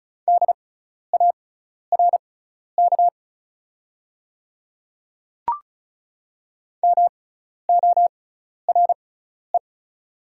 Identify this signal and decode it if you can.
Morse code, heard as a keyed tone, about 36 wpm, spaced as for 12 wpm, 690 Hz. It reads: DARK MORE
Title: 6x Character Spacing - Top 500 Words - 35wpm